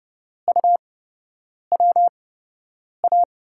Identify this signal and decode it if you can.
Morse code, heard as a keyed tone, about 30 wpm, spaced as for 8 wpm, 700 Hz. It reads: UWA
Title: USA Call Signs - 6 Characters - 8x Character Spacing - 30wpm